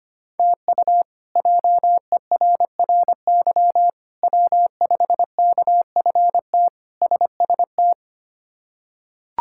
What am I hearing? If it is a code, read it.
Morse code, heard as a keyed tone, about 25 wpm, 700 Hz. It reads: TU JERRY W5KFT SST
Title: K1USN SST QSO Practice - 25wpm